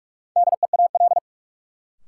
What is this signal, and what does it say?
Morse code, heard as a keyed tone, about 48 wpm, 700 Hz. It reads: DEAL
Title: Sets of 3 Words - New Word Drill - 45wpm